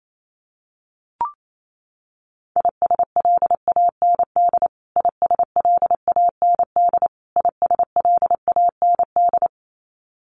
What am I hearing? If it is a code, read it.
Morse code, heard as a keyed tone, about 28 wpm, 700 Hz. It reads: ISLAND ISLAND ISLAND
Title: Top 500 Words Repeated 28wpm